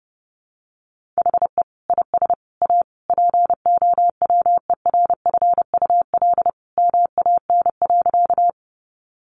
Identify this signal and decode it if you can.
Morse code, heard as a keyed tone, about 30 wpm, 700 Hz. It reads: HE IS A POWERFUL MAN.
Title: Sentences from Top 1300 Words 30wpm